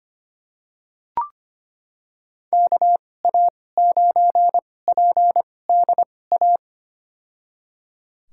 Morse code, heard as a keyed tone, about 25 wpm, spaced as for 18 wpm, 700 Hz. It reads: KA9PDA
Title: USA Call Signs - 6 Characters - 2x Character Spacing - 25wpm